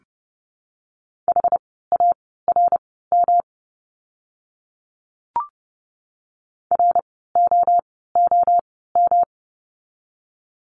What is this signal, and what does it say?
Morse code, heard as a keyed tone, about 30 wpm, spaced as for 17 wpm, 700 Hz. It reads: HARM ROOM
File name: course / 30:10wpm / 108 Lesson 18f - Letter m - Words 30wpm